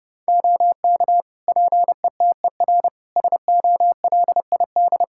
Morse code, heard as a keyed tone, about 31 wpm, 700 Hz. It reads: OK PETER SOLID
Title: K1USN SST QSO Practice 30wpm